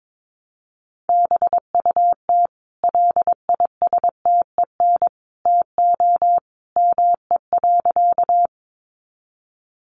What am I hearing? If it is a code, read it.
Morse code, heard as a keyed tone, about 22 wpm, 700 Hz. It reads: BUT LISTEN TO ME.
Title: Sentences from Top 400 Words 22wpm